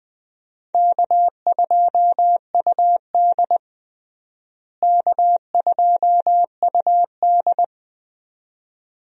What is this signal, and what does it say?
Morse code, heard as a keyed tone, about 21 wpm, 700 Hz. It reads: K2UD K2UD
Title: USA Call Signs - 4 Characters - Repeated 2x - 20wpm